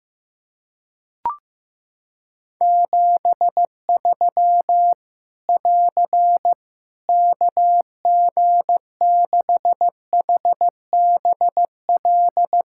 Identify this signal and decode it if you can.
Morse code, heard as a keyed tone, about 15 wpm, 700 Hz. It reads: 73 <AR> KG6HBL
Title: Practice QSO - Exchange 4 - Ending - 15wpm